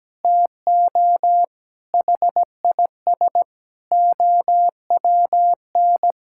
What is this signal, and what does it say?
Morse code, heard as a keyed tone, about 17 wpm, 700 Hz. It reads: TO HIS OWN
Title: Sentences from Top 1000 Words - Review Part 6 - 17wpm